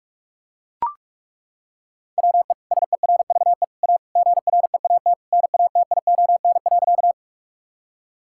Morse code, heard as a keyed tone, about 41 wpm, 700 Hz. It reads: WE SERVE A GREAT NATION.
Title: Sentences from Top 1000 Words - Review Part 6 - 45wpm